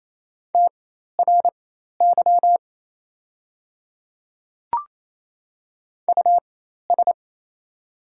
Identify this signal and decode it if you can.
Morse code, heard as a keyed tone, about 28 wpm, spaced as for 13 wpm, 690 Hz. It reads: TRY US
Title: 4x Character Spacing - Top 200 Words - 28wpm